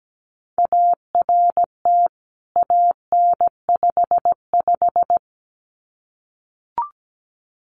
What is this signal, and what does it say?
Morse code, heard as a keyed tone, about 17 wpm, 700 Hz. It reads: ART AN55
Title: CWT Exchange 17wpm